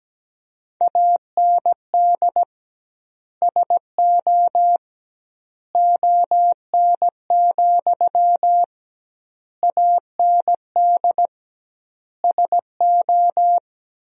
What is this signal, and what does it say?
Morse code, heard as a keyed tone, about 17 wpm, 700 Hz. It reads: AND SO ON, AND SO
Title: 2x Word-Spacing - Sentences from Top 100 Words - 17wpm